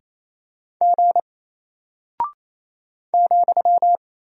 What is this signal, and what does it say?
Morse code, heard as a keyed tone, about 27 wpm, 700 Hz. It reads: G ,
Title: Single Character (Letter:Number:Punctuation:Prosign) Alt 28wpm